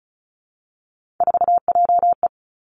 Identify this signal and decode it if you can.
Morse code, heard as a keyed tone, about 36 wpm, 700 Hz. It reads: 4JE
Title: Three Challenging Letter-Number Combinations 35wpm